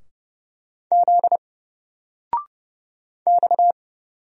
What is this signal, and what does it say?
Morse code, heard as a keyed tone, about 30 wpm, 700 Hz. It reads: Z X
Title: Single Character (Letter:Number:Punctuation:Prosign) 30wpm